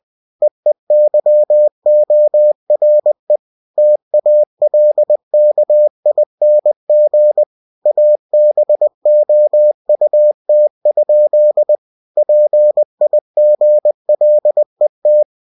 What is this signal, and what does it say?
Morse code, heard as a keyed tone, about 20 wpm, 600 Hz. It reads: EEYORE TALKING ABOUT? PIGLET
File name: cw-books / winnie-the-pooh / winnie-20wpm-page-0108